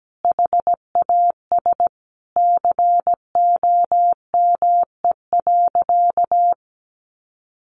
Morse code, heard as a keyed tone, about 17 wpm, 700 Hz. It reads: HAS COME.
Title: Sentences from Top 1100 Words 17wpm